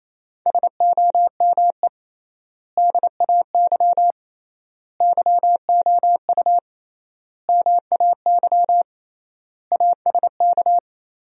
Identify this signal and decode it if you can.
Morse code, heard as a keyed tone, about 28 wpm, 700 Hz. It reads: SOME DAY YOU MAY ASK
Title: 3x Word-Spacing - Sentences from Top 200 Words - 28wpm